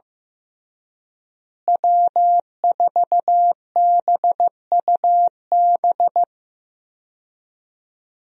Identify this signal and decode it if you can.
Morse code, heard as a keyed tone, about 15 wpm, 700 Hz. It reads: W4BUB
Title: USA Call Signs - 5 Characters - Repeated 3x - 15wpm